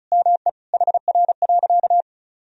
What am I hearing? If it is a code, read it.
Morse code, heard as a keyed tone, about 34 wpm, 700 Hz. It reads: ME HR.
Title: Sentences from Top 200 Words and Abbreviations 35wpm